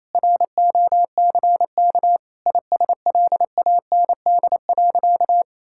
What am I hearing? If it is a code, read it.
Morse code, heard as a keyed tone, about 28 wpm, 700 Hz. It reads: ROCK ISLAND.
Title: Sentences from Top 1600 Words 28wpm